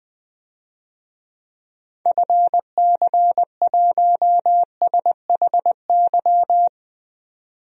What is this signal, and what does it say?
Morse code, heard as a keyed tone, about 20 wpm, 700 Hz. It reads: FC1SHY
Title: DXCC-100 Call Signs - 6 Characters - 20wpm